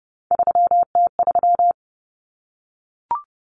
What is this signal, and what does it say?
Morse code, heard as a keyed tone, about 30 wpm, 700 Hz. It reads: 3T3
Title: Common Cut Numbers - 3-Digit 30wpm